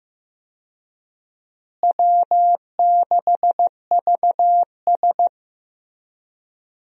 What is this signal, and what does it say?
Morse code, heard as a keyed tone, about 15 wpm, 700 Hz. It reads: W6VS
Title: USA Call Signs - 4 Characters - Repeated 3x - 15wpm